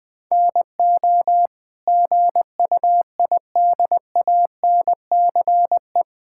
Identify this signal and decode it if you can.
Morse code, heard as a keyed tone, about 20 wpm, 700 Hz. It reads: NO GUIDANCE
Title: Sentences from Top 1600 Words 20wpm